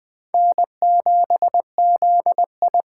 Morse code, heard as a keyed tone, about 20 wpm, 700 Hz. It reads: N7ZI
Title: Speed Racing - USA 1x2 Call Signs 17wpm